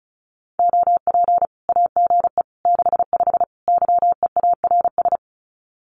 Morse code, heard as a keyed tone, about 34 wpm, 700 Hz. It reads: OP AGE 65 YEARS
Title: Common QSO Phrases 35wpm